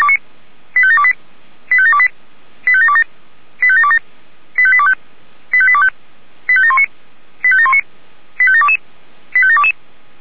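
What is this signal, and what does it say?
Five-tone selective call (ZVEI), heard as quick runs of five tones, tones of about 70 ms. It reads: 85728 85728 85727 85727 85825 85825 85719 85719 85722 85722
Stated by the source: ZVEI five-tone selective call received on SDR.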